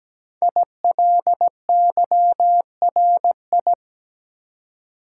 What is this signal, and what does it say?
Morse code, heard as a keyed tone, about 17 wpm, 700 Hz. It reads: ILYRI
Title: Code Groups - 5 Letters - 17wpm